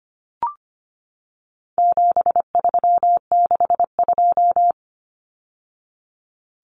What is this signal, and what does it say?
Morse code, heard as a keyed tone, about 25 wpm, 700 Hz. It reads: 7362
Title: Numbers - 4-Digit 25wpm